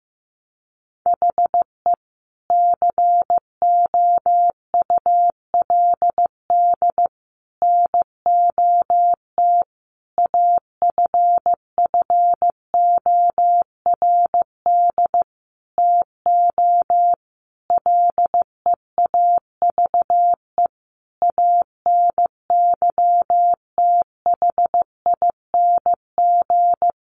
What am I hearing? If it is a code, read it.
Morse code, heard as a keyed tone, about 14 wpm, 700 Hz. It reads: HE COULD NOT AFFORD TO LEAVE ANYTHING
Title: Sentences from Top 1100 Words 15wpm